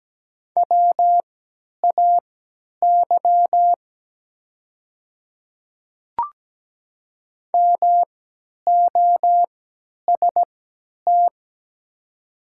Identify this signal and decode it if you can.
Morse code, heard as a keyed tone, about 17 wpm, spaced as for 10 wpm, 700 Hz. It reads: WAY MOST
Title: 3x Character Spacing - Top 100 Words - 17wpm